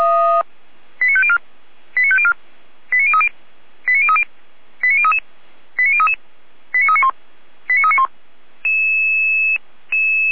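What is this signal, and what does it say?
Five-tone selective call (ZVEI), heard as quick runs of five tones, tones of about 70 ms. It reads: 89504 89504 89030 89030 89033 89033 89391 89391
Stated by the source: ZVEI five-tone selective call received on an SDR.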